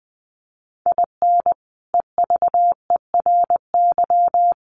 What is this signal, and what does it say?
Morse code, heard as a keyed tone, about 20 wpm, 700 Hz. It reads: IN EVERY
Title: Sentences from Top 1400 Words 20wpm